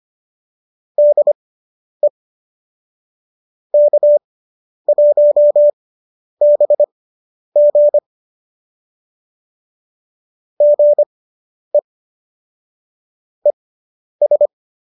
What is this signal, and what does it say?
Morse code, heard as a keyed tone, about 25 wpm, spaced as for 10 wpm, 590 Hz. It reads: DE K1BG GE ES